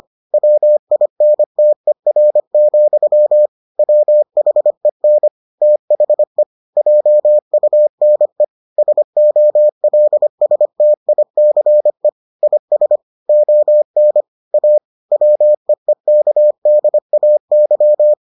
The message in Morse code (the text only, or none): WINTER, WHEN THE JUNE SOLSTICE IS ON A WEEKDAY